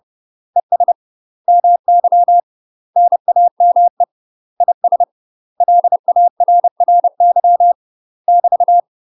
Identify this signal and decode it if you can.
Morse code, heard as a keyed tone, about 30 wpm, 700 Hz. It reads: ES MY NAME IS LARRY <BT>
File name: UK MASTER QSOs TRAINING / 30WPM / UK MASTER QSOs 0008